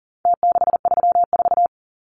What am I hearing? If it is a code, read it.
Morse code, heard as a keyed tone, about 40 wpm, 700 Hz. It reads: T634
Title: Numbers - Contest Style - 4-Digit 40wpm